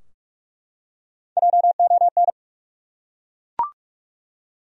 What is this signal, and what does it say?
Morse code, heard as a keyed tone, about 46 wpm, 700 Hz. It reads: JON
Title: USA Common Names - 45wpm